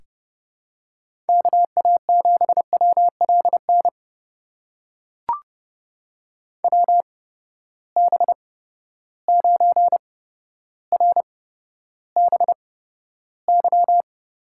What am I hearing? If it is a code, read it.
Morse code, heard as a keyed tone, about 30 wpm, spaced as for 13 wpm, 700 Hz. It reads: KA7WLN WB9RBY
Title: USA Call Signs - 6 Characters - 8x Character Spacing - 30wpm